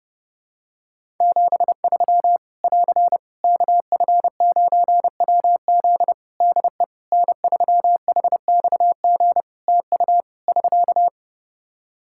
Morse code, heard as a keyed tone, about 29 wpm, 700 Hz. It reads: 73 <AR> KF9WZ DE N3HXG TU <SK>
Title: Practice QSO - Exchange 4 - Ending - 30wpm